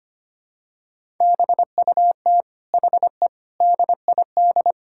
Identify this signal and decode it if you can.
Morse code, heard as a keyed tone, about 25 wpm, 700 Hz. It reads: BUT HE DID
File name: practice / 25wpm / Sentences with 4-Word Phrases - 25wpm